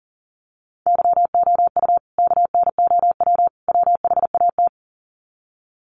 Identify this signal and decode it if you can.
Morse code, heard as a keyed tone, about 37 wpm, 700 Hz. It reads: YOU KNOW WHAT
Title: Sets of 3 Words 40wpm